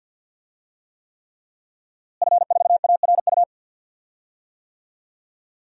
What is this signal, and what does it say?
Morse code, heard as a keyed tone, about 51 wpm, 700 Hz. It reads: F4ARU
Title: DXCC-100 Call Signs - 5 Characters - Repeated 3x - 50wpm